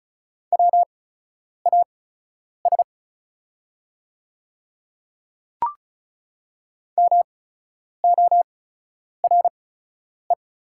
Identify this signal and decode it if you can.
Morse code, heard as a keyed tone, about 36 wpm, spaced as for 10 wpm, 700 Hz. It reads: WAS MORE
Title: 8x Character Spacing - Top 200 Words - 35wpm